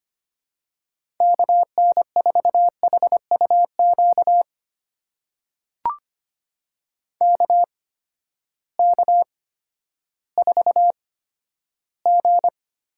Morse code, heard as a keyed tone, about 25 wpm, spaced as for 12 wpm, 700 Hz. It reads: KN4HUQ KK4G
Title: USA Call Signs - 6 Characters - 8x Character Spacing - 25wpm